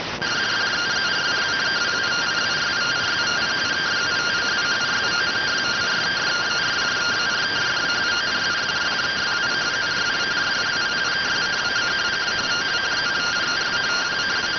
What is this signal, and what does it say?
Two-tone teleprinter signal, about 170 Hz shift.